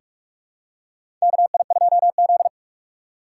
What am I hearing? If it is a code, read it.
Morse code, heard as a keyed tone, about 46 wpm, 700 Hz. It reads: KI2Z
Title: USA Call Signs - 4 Characters - 4x Character Spacing - 45wpm